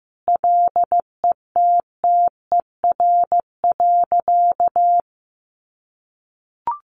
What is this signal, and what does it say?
Morse code, heard as a keyed tone, about 14 wpm, 700 Hz. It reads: LETTER.
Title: Sentences from Top 1400 Words 15wpm